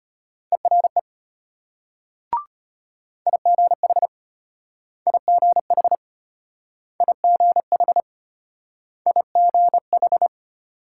Morse code, heard as a keyed tone, about 34 wpm, 700 Hz. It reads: ERE IGH IGH IGH IGH